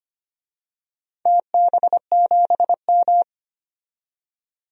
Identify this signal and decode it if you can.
Morse code, heard as a keyed tone, about 25 wpm, 700 Hz. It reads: TB7M
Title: DXCC-100 Call Signs - 4 Characters - 25wpm